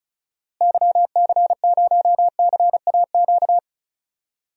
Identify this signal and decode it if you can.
Morse code, heard as a keyed tone, about 34 wpm, 700 Hz. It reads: YC0CAQ
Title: DXCC-100 Call Signs - 6 Characters - Repeated 2x - 35wpm